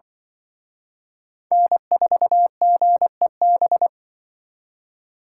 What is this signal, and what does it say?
Morse code, heard as a keyed tone, about 24 wpm, 700 Hz. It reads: N4GEB
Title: Speed Racing - USA Call Signs 20wpm